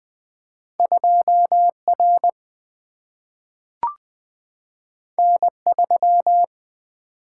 Morse code, heard as a keyed tone, about 20 wpm, 700 Hz. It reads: 2R N3
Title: Two Letter-Number Combinations - 20wpm